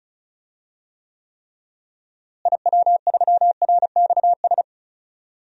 Morse code, heard as a keyed tone, about 36 wpm, 700 Hz. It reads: IW3RXS
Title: DXCC-100 Call Signs - 6 Characters - Repeated 3x - 35wpm